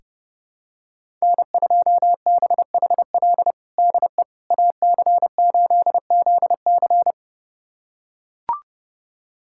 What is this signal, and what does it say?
Morse code, heard as a keyed tone, about 30 wpm, 700 Hz. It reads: N2BHL DE AC8ZC
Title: USA Callsign DE Callsign - 5 Characters - 30wpm